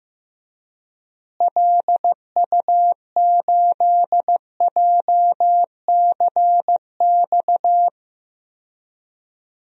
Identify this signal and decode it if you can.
Morse code, heard as a keyed tone, about 15 wpm, 700 Hz. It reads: LU8JCX
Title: DXCC-100 Call Signs - 6 Characters - Repeated 3x - 15wpm